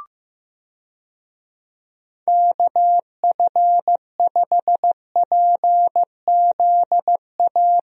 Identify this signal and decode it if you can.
Morse code, heard as a keyed tone, about 15 wpm, 700 Hz. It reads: KF5PZA
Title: USA Call Signs - 6 Characters - Repeated 3x - 15wpm